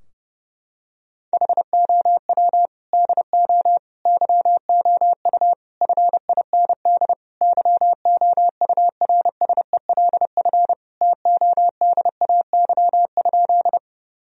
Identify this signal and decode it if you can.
Morse code, heard as a keyed tone, about 32 wpm, 700 Hz. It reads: HOW DO YOU FIND YOURSELF TODAY?
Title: Sentences from Top 1000 Words - Review Part 4 - 30wpm